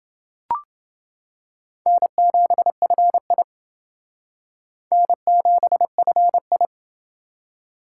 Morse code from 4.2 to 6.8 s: N7FI

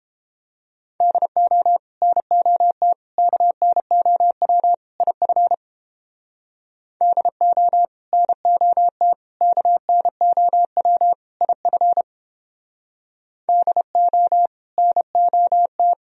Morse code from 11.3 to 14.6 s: IF DO

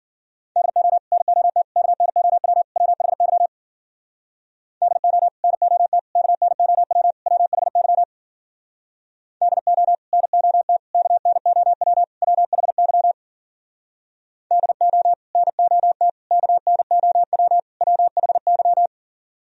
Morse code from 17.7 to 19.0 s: WHY

